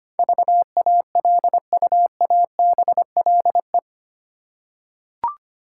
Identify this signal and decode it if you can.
Morse code, heard as a keyed tone, about 25 wpm, 700 Hz. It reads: VALUABLE
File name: Eight Letter Words - 25wpm